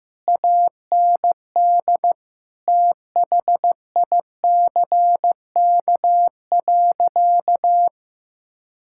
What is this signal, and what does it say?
Morse code, heard as a keyed tone, about 16 wpm, 700 Hz. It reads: AND THICK.
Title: Sentences from Top 1600 Words 15wpm